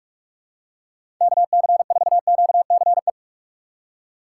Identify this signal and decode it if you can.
Morse code, heard as a keyed tone, about 43 wpm, 700 Hz. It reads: KC4QCE
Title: USA Call Signs - 6 Characters - 6x Character Spacing - 45wpm